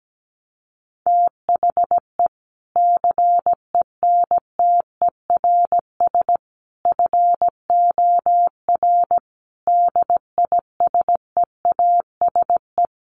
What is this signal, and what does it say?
Morse code, heard as a keyed tone, about 17 wpm, 700 Hz. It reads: THE CENTERS FOR DISEASE
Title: Sets of 5 Words Encore2 17wpm